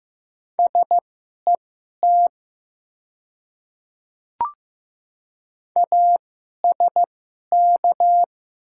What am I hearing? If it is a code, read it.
Morse code, heard as a keyed tone, about 15 wpm, spaced as for 11 wpm, 690 Hz. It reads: SET ASK
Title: 2x Character Spacing - Top 200 Words - 15wpm